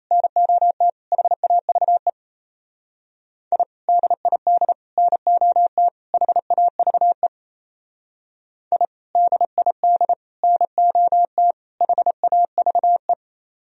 Morse code, heard as a keyed tone, about 36 wpm, 700 Hz. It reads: NOT HAVE I DID NOT HAVE I DID NOT HAVE